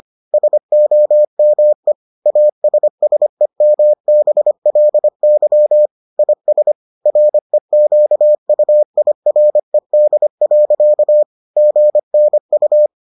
Morse code, heard as a keyed tone, about 25 wpm, 600 Hz. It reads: SOME ASSEMBLY IS REQUIRED. GNU